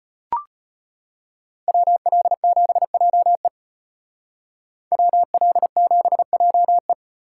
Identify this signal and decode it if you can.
Morse code, heard as a keyed tone, about 33 wpm, 700 Hz. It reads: WL7JE WL7JE